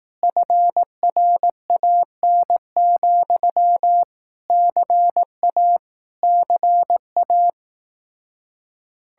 Morse code, heard as a keyed tone, about 18 wpm, 700 Hz. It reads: FRAN, CA CA
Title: Speed-Racing Common QSO Phrases Encore - 15wpm